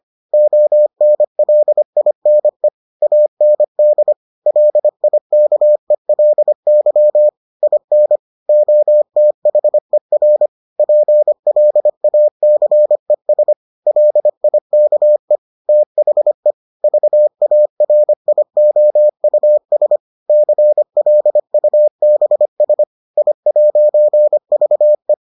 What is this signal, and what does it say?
Morse code, heard as a keyed tone, about 25 wpm, 600 Hz. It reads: ONLINE AND LIKELY IN OTHER PLACES LIKE THE VARIOUS CLUBS I'VE